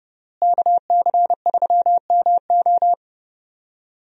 Morse code, heard as a keyed tone, about 30 wpm, 700 Hz. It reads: KC3MO
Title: USA Call Signs - 5 Characters - Repeated 3x - 30wpm